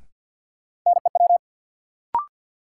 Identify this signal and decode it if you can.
Morse code, heard as a keyed tone, about 50 wpm, 700 Hz. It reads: NEW
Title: Sentences from Top 300 Words - New Word Drill - 50wpm